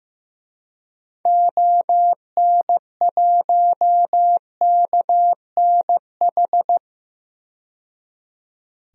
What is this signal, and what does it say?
Morse code, heard as a keyed tone, about 15 wpm, 700 Hz. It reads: ON1KNH
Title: DXCC-100 Call Signs - 6 Characters - 15wpm